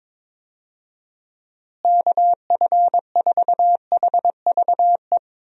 Morse code, heard as a keyed tone, about 22 wpm, 700 Hz. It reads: KF4HVE
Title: USA Call Signs - 6 Characters - Repeated 2x - 22wpm